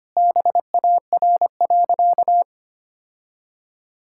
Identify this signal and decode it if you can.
Morse code, heard as a keyed tone, about 25 wpm, 700 Hz. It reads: BAR.
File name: practice / 25wpm / All-Age Appropriate Jokes - 25wpm